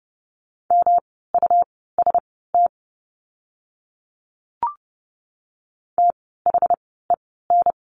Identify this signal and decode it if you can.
Morse code, heard as a keyed tone, about 30 wpm, spaced as for 17 wpm, 700 Hz. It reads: MUST THEN
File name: course / 30:10wpm / 237 Lesson 38f - Letter q - Words 30wpm